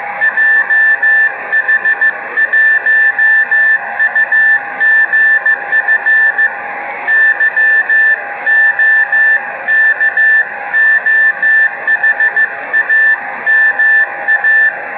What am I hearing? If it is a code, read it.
Morse code, heard as a keyed tone, about 15 wpm, 1750 Hz, with noise. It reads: JH1UGF YOKOHAMA
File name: Beacon / 5GbeaconJH1UGF awasaki